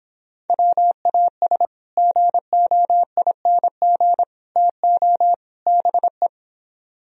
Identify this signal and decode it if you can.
Morse code, heard as a keyed tone, about 26 wpm, 700 Hz. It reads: WAS GOING TO BE